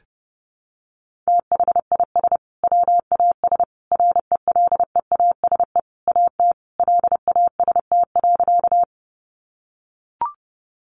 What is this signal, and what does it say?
Morse code, heard as a keyed tone, about 30 wpm, 700 Hz. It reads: THIS WAS RELEASE AT LAST.
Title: Sentences from Top 1100 Words 30wpm